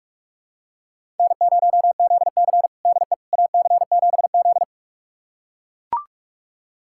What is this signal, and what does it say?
Morse code, heard as a keyed tone, about 42 wpm, 700 Hz. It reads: N0ZC DE AC7Z
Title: USA Callsign DE Callsign - 4 Characters - 45wpm